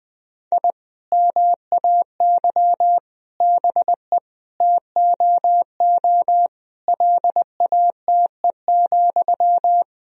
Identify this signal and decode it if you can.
Morse code, heard as a keyed tone, about 20 wpm, 700 Hz. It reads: I MAY BE TOO LATE,
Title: Sentences from Top 1000 Words - Review Part 1 - 20wpm